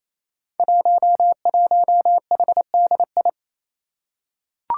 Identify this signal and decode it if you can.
Morse code, heard as a keyed tone, about 28 wpm, 700 Hz. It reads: 11HDI
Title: Five Challenging Letter-Number Combinations 28wpm